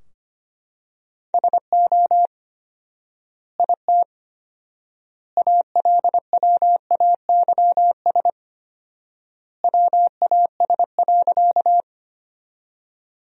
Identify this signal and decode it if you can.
Morse code, heard as a keyed tone, about 25 wpm, 700 Hz. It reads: SO IT ALWAYS WAS.
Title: 4x Word-Spacing - Sentences from Top 300 Words - 25wpm